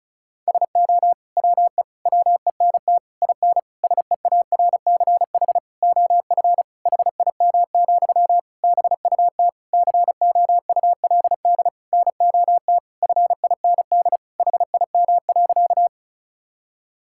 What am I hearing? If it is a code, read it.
Morse code, heard as a keyed tone, about 36 wpm, 700 Hz. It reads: SO WE WENT IN SEARCH OF HIM, BUT COULD NOT FIND HIM.
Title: Sentences from Top 1000 Words - Review Part 5 - 35wpm